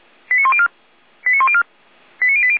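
Five-tone selective call (ZVEI), heard as quick runs of five tones, tones of about 70 ms. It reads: 89104 89104 89080
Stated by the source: ZVEI five-tone call received by SDR.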